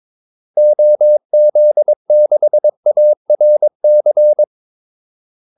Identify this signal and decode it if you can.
Morse code, heard as a keyed tone, about 22 wpm, 600 Hz. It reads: OZ6ARC
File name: Callsigns-22wpm